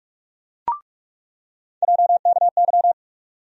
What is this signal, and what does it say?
Morse code, heard as a keyed tone, about 44 wpm, 700 Hz. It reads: JKY
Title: USA Call Sign Postfixes - 3 Letters 45wpm